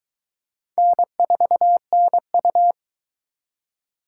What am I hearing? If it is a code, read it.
Morse code, heard as a keyed tone, about 24 wpm, 700 Hz. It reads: N4NU